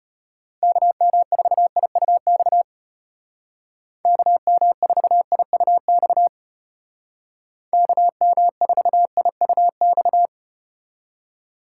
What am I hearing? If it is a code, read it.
Morse code, heard as a keyed tone, about 34 wpm, 700 Hz. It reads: KM4IUX KM4IUX KM4IUX